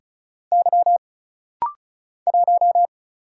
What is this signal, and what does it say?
Morse code, heard as a keyed tone, about 36 wpm, 700 Hz. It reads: Y 1
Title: Single Letter-Number - Rapid-Fire 35wpm